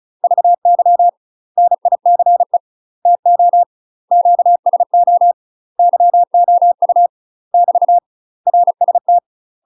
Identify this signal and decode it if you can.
Morse code, heard as a keyed tone, about 34 wpm, 700 Hz. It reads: VY NICE TO QSO YOU <BT> RST
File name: UK MASTER QSOs TRAINING / 35WPM / UK MASTER QSOs 0024